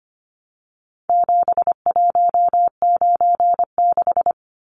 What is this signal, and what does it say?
Morse code, heard as a keyed tone, about 25 wpm, 700 Hz. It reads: 7196
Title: Numbers - 4-Digit 25wpm